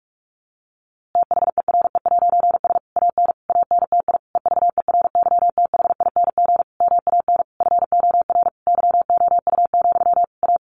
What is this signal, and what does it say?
Morse code, heard as a keyed tone, about 44 wpm, 700 Hz. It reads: THERE'S AN ANTI EVERYTHING MAN FOR YOU, A
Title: A Sound Of Thunder 45wpm